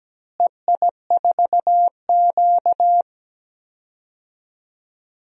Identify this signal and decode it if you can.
Morse code, heard as a keyed tone, about 17 wpm, 700 Hz. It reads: EI4Q
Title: DXCC-100 Call Signs - 4 Characters - Repeated 3x - 17wpm